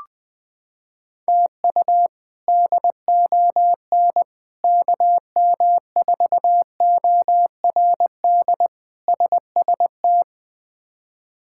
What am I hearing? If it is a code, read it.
Morse code, heard as a keyed tone, about 21 wpm, 700 Hz. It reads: TU DON KM4ORD SST
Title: K1USN SST QSO Practice - 20wpm